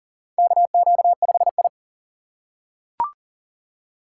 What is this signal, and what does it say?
Morse code, heard as a keyed tone, about 40 wpm, 700 Hz. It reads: KQ5I
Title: USA Call Signs - 4 Characters - 2x Character Spacing - 40wpm